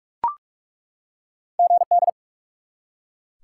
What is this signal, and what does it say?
Morse code, heard as a keyed tone, about 46 wpm, 700 Hz. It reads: GD
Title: Top 90 QSO Elements 45wpm